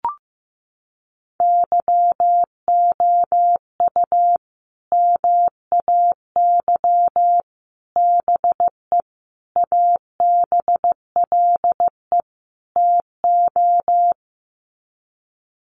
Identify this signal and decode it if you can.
Morse code, heard as a keyed tone, about 15 wpm, 700 Hz. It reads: YOU MAY BE ABLE TO
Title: Sets of 5 Words Encore 15wpm